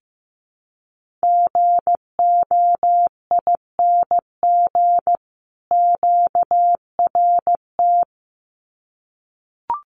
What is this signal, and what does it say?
Morse code, heard as a keyed tone, about 15 wpm, 700 Hz. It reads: GOING QRT
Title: Common QSO Phrases 15wpm